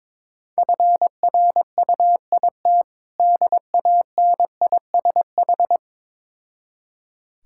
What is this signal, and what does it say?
Morse code, heard as a keyed tone, about 22 wpm, 690 Hz. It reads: FRUIT DANISH
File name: Difficult Words with Hints - 22wpm